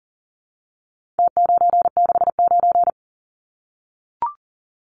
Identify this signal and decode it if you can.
Morse code, heard as a keyed tone, about 40 wpm, 700 Hz. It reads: T969
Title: Numbers - Contest Style - 4-Digit 40wpm